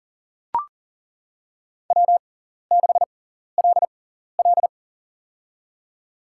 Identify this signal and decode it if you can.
Morse code, heard as a keyed tone, about 40 wpm, spaced as for 14 wpm, 700 Hz. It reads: W6LL
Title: USA Call Signs - 4 Characters - 6x Character Spacing - 40wpm